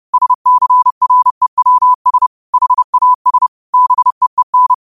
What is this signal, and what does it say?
Morse code, heard as a keyed tone, about 30 wpm, 1000 Hz. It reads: SCREWS HAS BEEN
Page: New quotes every day in morse code at 30 Words per minute.